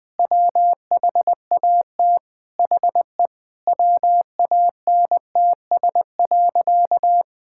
Morse code, heard as a keyed tone, about 20 wpm, 700 Hz. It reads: WHAT HE WANTS.
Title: Sentences with 2-Word Phrases - 20wpm